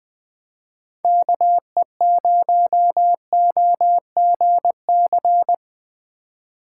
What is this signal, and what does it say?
Morse code, heard as a keyed tone, about 20 wpm, 700 Hz. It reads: KE0OGC
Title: Speed Racing - USA 2x3 Call Signs 17wpm